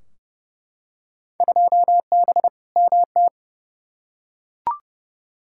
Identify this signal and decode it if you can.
Morse code, heard as a keyed tone, about 30 wpm, 700 Hz. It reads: 2B MT
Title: Field Day Exchanges Encore - 30wpm